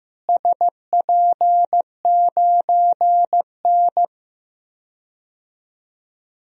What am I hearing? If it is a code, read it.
Morse code, heard as a keyed tone, about 15 wpm, 700 Hz. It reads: SP9N
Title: DXCC-100 Call Signs - 4 Characters - Repeated 2x - 15wpm